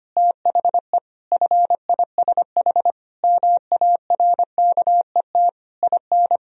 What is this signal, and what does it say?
Morse code, heard as a keyed tone, about 25 wpm, 700 Hz. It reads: THE FISH MARKET IN
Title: Sentences from Top 1000 Words - Review Part 3 - 25wpm